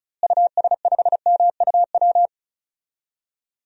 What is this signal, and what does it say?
Morse code, heard as a keyed tone, about 36 wpm, 700 Hz. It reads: US5MUW
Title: DXCC-100 Call Signs - 6 Characters - Repeated 3x - 35wpm